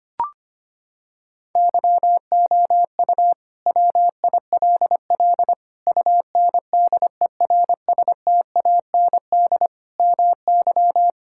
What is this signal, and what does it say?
Morse code, heard as a keyed tone, about 25 wpm, 700 Hz. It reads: YOU WILL UNDERSTAND MY
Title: Sentences from Top 1600 Words 25wpm